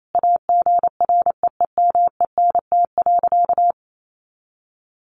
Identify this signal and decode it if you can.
Morse code, heard as a keyed tone, about 28 wpm, 700 Hz. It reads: AGREEMENT.
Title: Sentences from Top 1500 Words 28wpm